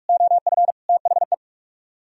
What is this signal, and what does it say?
Morse code, heard as a keyed tone, about 60 wpm, spaced as for 52 wpm, 700 Hz. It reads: OF THE
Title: Speed-Racing - Sets of 4 Words 45wpm